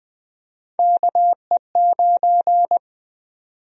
Morse code, heard as a keyed tone, about 20 wpm, 700 Hz. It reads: KE9
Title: USA Call Sign Prefixes - 20wpm